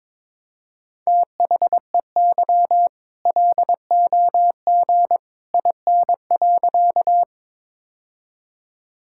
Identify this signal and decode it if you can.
Morse code, heard as a keyed tone, about 22 wpm, 690 Hz. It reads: THEY LOG IN.
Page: morse-code-ninja-practice-22wpm